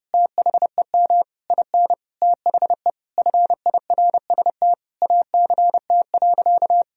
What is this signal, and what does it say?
Morse code, heard as a keyed tone, about 30 wpm, 700 Hz. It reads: THEM IN THE FIRST ACT.
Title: Sentences from Top 1000 Words - Review Part 2 - 30wpm